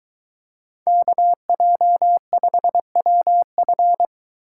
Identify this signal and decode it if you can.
Morse code, heard as a keyed tone, about 22 wpm, 700 Hz. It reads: KJ5WF